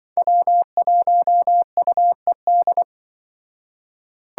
Morse code, heard as a keyed tone, about 23 wpm, 700 Hz. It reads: W1UED
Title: Speed Racing - USA 1x3 Call Signs 20wpm